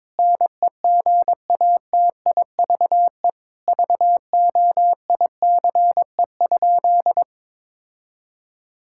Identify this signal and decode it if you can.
Morse code, heard as a keyed tone, about 22 wpm, 700 Hz. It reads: NEGATIVE VOICE?
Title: Sentences from Top 1000 Words - Review Part 6 - 22wpm